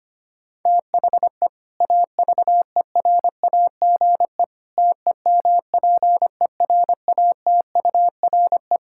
Morse code, heard as a keyed tone, about 25 wpm, 700 Hz. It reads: THE AVERAGE TEMPERATURE
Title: Sentences from Top 1000 Words - Review Part 6 - 25wpm